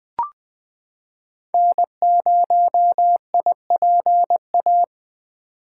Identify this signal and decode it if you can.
Morse code, heard as a keyed tone, about 20 wpm, 700 Hz. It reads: N0IPA
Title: 2023 SOTA Chaser Call Signs - 20wpm